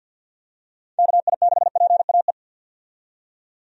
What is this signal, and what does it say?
Morse code, heard as a keyed tone, about 45 wpm, 700 Hz. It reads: KI6PAE
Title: USA Call Signs - 6 Characters - 50wpm